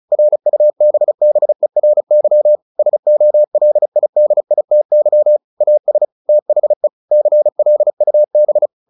Morse code, heard as a keyed tone, about 34 wpm, 600 Hz. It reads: RUBBERY SOLIDITY AS THE CLUB
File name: the old man and the sea 2 35wpm